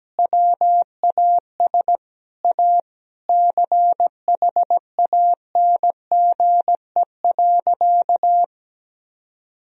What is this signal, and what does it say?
Morse code, heard as a keyed tone, about 17 wpm, 700 Hz. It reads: WAS A CHANGE.
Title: Sentences from Top 1000 Words - Review Part 1 - 17wpm